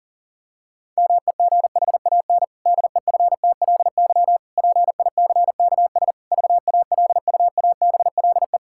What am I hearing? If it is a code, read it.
Morse code, heard as a keyed tone, about 37 wpm, 700 Hz. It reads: MEGHAN DEFTLY PICKS VALUABLE
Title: Pangram Challenge v2 - 40wpm